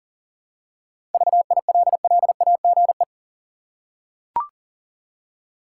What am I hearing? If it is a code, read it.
Morse code, heard as a keyed tone, about 40 wpm, 700 Hz. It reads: VILLAGE
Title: Seven Letter Words - 40wpm